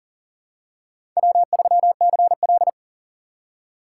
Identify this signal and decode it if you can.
Morse code, heard as a keyed tone, about 40 wpm, 700 Hz. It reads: W3CL
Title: USA Call Signs - 4 Characters - 6x Character Spacing - 40wpm